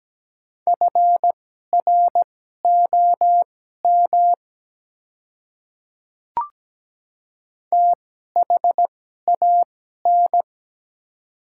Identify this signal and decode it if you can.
Morse code, heard as a keyed tone, about 17 wpm, spaced as for 12 wpm, 700 Hz. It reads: FROM THAN
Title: 2x Character Spacing - Top 100 Words - 17wpm